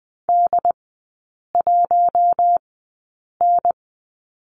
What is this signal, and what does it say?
Morse code, heard as a keyed tone, about 20 wpm, spaced as for 8 wpm, 700 Hz. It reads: D1N
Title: 093 Lesson 16e - Letter c - Three Characters 20wpm